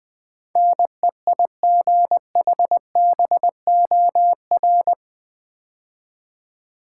Morse code, heard as a keyed tone, about 20 wpm, 700 Hz. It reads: NEIGHBOR
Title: Build A Word - 4 Letters At A Time - 11-Letter Words - 20wpm